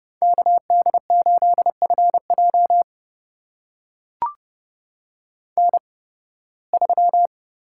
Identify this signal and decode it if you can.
Morse code, heard as a keyed tone, about 30 wpm, spaced as for 20 wpm, 700 Hz. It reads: KD8FJ N3
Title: USA Call Signs - 5 Characters - 8x Character Spacing - 30wpm